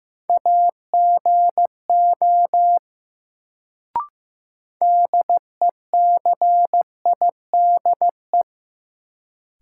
Morse code, heard as a keyed tone, about 15 wpm, 700 Hz. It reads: AGO DECIDE
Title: Sentences from Top 500 Words - New Word Drill - 15wpm